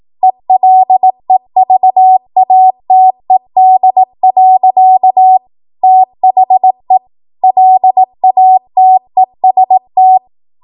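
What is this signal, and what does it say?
Morse code, heard as a keyed tone, about 18 wpm, 750 Hz. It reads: ELEVATED. THE LATEST
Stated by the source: ARRL 18 wpm Morse code practice archive transmission.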